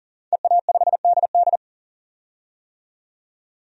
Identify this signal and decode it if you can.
Morse code, heard as a keyed tone, about 40 wpm, 700 Hz. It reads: EA5DD
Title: DXCC-100 Call Signs - 5 Characters - 40wpm